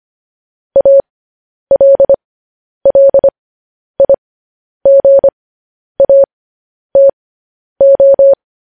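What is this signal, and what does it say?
Morse code, heard as a keyed tone, about 25 wpm, spaced as for 10 wpm, 550 Hz. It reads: ALLIGATO
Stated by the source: Morse news headlines sent at 10 words per minute.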